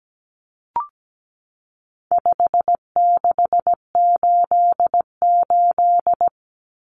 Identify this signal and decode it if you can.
Morse code, heard as a keyed tone, about 17 wpm, 700 Hz. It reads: 5688
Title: Common Cut Numbers - 4-Digit 17wpm